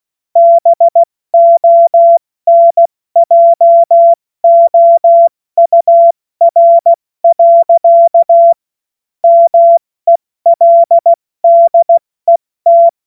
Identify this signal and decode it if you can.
Morse code, heard as a keyed tone, about 16 wpm, 680 Hz. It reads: BONJOUR. MELDET
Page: cw-witze